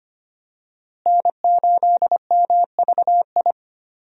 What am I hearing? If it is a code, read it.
Morse code, heard as a keyed tone, about 25 wpm, 700 Hz. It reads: N8MVI
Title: USA Call Signs - 5 Characters - Repeated 2x - 25wpm